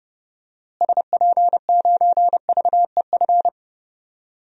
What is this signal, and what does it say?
Morse code, heard as a keyed tone, about 30 wpm, 700 Hz. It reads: SP9VEF
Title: DXCC-100 Call Signs - 6 Characters - Repeated 2x - 30wpm